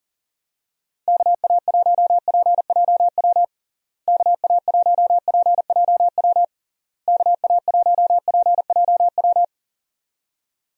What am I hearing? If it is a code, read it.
Morse code, heard as a keyed tone, about 40 wpm, 700 Hz. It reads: KA1PJW KA1PJW KA1PJW
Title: USA Call Signs - 6 Characters - Repeated 3x - 40wpm